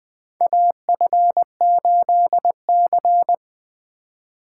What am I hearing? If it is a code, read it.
Morse code, heard as a keyed tone, about 20 wpm, 700 Hz. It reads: AF8C
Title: USA Call Signs - 4 Characters - 8x Character Spacing - 20wpm